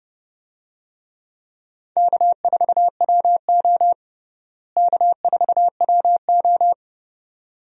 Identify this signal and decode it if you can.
Morse code, heard as a keyed tone, about 32 wpm, 700 Hz. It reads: K4WO K4WO
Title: USA Call Signs - 4 Characters - Repeated 2x - 30wpm